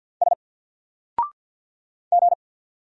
Morse code, heard as a keyed tone, about 49 wpm, 700 Hz. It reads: S G
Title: Single Character (Letter:Number:Punctuation:Prosign) Alt 50wpm